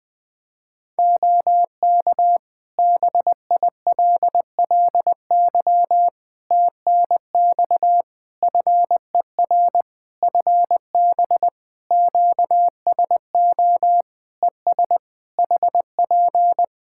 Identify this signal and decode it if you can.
Morse code, heard as a keyed tone, about 20 wpm, 700 Hz. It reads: OK BILLY TNX FER FB QSO ES HP
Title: Practice QSO - Exchange 4 - Ending - 20wpm